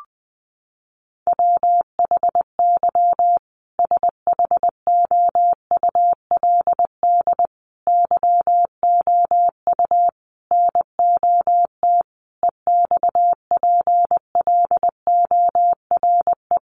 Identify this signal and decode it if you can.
Morse code, heard as a keyed tone, about 20 wpm, 700 Hz. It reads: WHY SHOULD YOU NOT EXPLORE
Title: Sentences from Top 1100 Words 20wpm